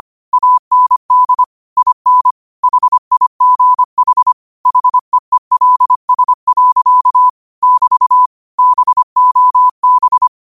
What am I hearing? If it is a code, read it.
Morse code, heard as a keyed tone, about 25 wpm, 1000 Hz. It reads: AND IN HIGH HEELS. <BT> BOB
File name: qotdfiles / 25wpm